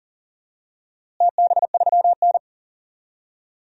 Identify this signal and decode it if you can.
Morse code, heard as a keyed tone, about 40 wpm, 700 Hz. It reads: TB3N
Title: DXCC-100 Call Signs - 4 Characters - Repeated 2x - 40wpm